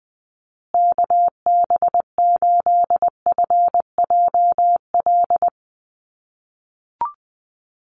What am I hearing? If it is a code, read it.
Morse code, heard as a keyed tone, about 19 wpm, 700 Hz. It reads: KB8FJL
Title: USA Call Signs Repeated 3x 20wpm